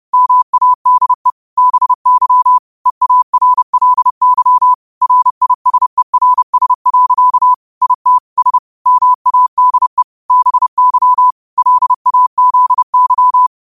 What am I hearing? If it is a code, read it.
Morse code, heard as a keyed tone, about 30 wpm, 1000 Hz. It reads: MADE BY EARLY RISERS. IT S MADE BY LAZY